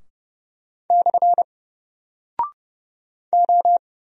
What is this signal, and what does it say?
Morse code, heard as a keyed tone, about 30 wpm, 700 Hz. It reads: / O
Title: Single Character (Letter:Number:Punctuation:Prosign) 30wpm